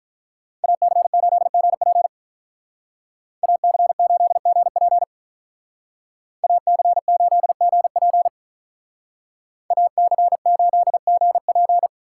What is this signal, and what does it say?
Morse code, heard as a keyed tone, about 43 wpm, 700 Hz. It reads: AC8GP AC8GP AC8GP AC8GP
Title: Speed Racing - USA 2x2 Call Signs 35wpm